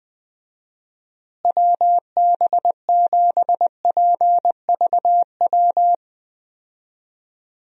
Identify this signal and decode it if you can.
Morse code, heard as a keyed tone, about 20 wpm, 700 Hz. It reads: WB7PVW
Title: USA Call Signs - 6 Characters - Repeated 2x - 20wpm